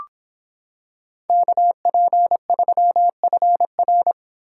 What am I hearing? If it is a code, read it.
Morse code, heard as a keyed tone, about 26 wpm, 700 Hz. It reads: KP3FR